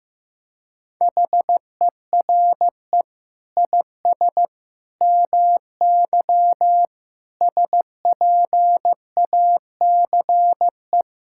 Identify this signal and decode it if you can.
Morse code, heard as a keyed tone, about 15 wpm, 700 Hz. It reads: HERE IS MY SPACE
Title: Sentences from Top 1000 Words - Review Part 5 - 15wpm